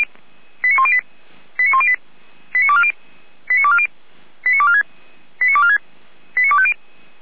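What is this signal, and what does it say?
Five-tone selective call (ZVEI), heard as quick runs of five tones, tones of about 80 ms. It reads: 89108 89108 89240 89240 89246 89246 89250
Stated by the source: Five-tone ZVEI selective call received on an SDR, from the Feuerwehr Orlamünde fire brigade audio archive.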